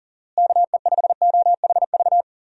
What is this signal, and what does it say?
Morse code, heard as a keyed tone, about 37 wpm, 700 Hz. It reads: KE5OHV
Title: USA Call Signs - 6 Characters - Repeated 3x - 40wpm